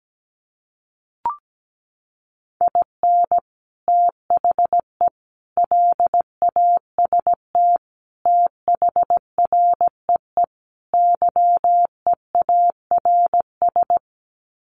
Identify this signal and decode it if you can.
Morse code, heard as a keyed tone, about 17 wpm, 700 Hz. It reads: IN THE LAST THREE YEARS
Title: Sets of 5 Words Encore2 17wpm